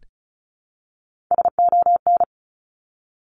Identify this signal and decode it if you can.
Morse code, heard as a keyed tone, about 36 wpm, 700 Hz. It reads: SON
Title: Top Three Letter Combinations in Words 35wpm